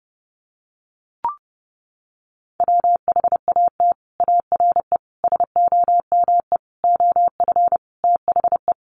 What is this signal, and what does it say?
Morse code, heard as a keyed tone, about 30 wpm, 700 Hz. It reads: WHAT ARE SOME OF THE
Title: Sets of 5 Words Encore 30wpm